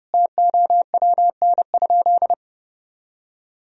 Morse code, heard as a keyed tone, about 30 wpm, 700 Hz. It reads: TOWN?
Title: Sentences from Top 1000 Words - Review Part 2 - 30wpm